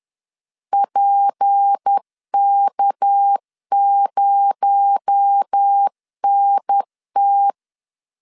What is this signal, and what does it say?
Morse code, heard as a keyed tone, about 10 wpm, 800 Hz. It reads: PK0NT
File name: LEZIONI / 6 PKX0 / 6 PKX0 10wpm 2707